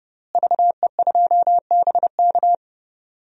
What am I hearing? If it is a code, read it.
Morse code, heard as a keyed tone, about 30 wpm, 700 Hz. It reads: VE2BK
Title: DXCC-100 Call Signs - 5 Characters - Repeated 3x - 30wpm